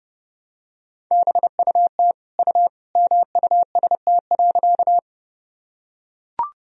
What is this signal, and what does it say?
Morse code, heard as a keyed tone, about 30 wpm, 700 Hz. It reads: BUT U MUST.
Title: Sentences from Top 200 Words and Abbreviations 30wpm